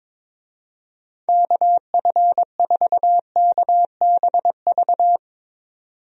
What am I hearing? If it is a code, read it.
Morse code, heard as a keyed tone, about 22 wpm, 700 Hz. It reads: KF4KBV
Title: USA Call Signs - 6 Characters - 22wpm